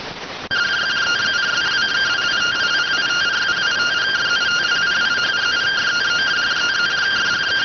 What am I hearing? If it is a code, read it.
Two-tone teleprinter signal, about 170 Hz shift.